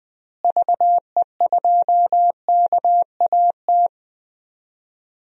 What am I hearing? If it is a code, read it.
Morse code, heard as a keyed tone, about 20 wpm, 700 Hz. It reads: VE2KAT
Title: DXCC-100 Call Signs - 6 Characters - Repeated 2x - 20wpm